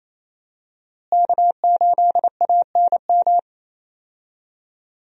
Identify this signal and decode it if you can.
Morse code, heard as a keyed tone, about 27 wpm, 700 Hz. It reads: K8ANM
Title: USA Call Signs - 5 Characters - Repeated 3x - 28wpm